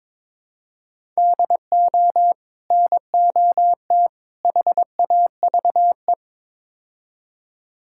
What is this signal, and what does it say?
Morse code, heard as a keyed tone, about 22 wpm, 700 Hz. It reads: DO NOT HAVE
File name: practice / 22wpm / Build A Phrase - 5-Word Phrases - 22wpm